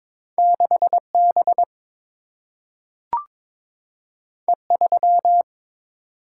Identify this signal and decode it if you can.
Morse code, heard as a keyed tone, about 22 wpm, 700 Hz. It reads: 6 B E 3
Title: Single-Letter Single-Number 22wpm